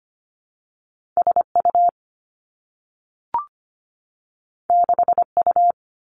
Morse code, heard as a keyed tone, about 25 wpm, 700 Hz. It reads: S U 6 U
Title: Two Challenging Letter-Number Combinations 25wpm